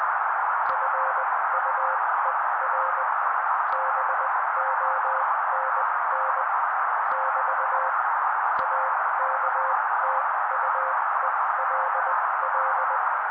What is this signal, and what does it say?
Morse code, heard as a keyed tone, about 20 wpm, spaced as for 16 wpm, 530 Hz. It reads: FUER BONN <BT> AKTUELL
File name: daily / 16wpm noise